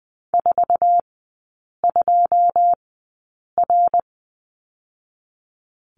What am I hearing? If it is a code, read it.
Morse code, heard as a keyed tone, about 20 wpm, spaced as for 8 wpm, 690 Hz. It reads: 42R
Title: 093 Lesson 16e - Letter c - Three Characters 20wpm